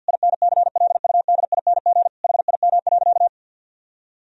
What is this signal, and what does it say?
Morse code, heard as a keyed tone, about 50 wpm, 700 Hz. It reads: INCLUDING HIM.
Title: Sentences from Top 1000 Words - Review Part 4 - 50wpm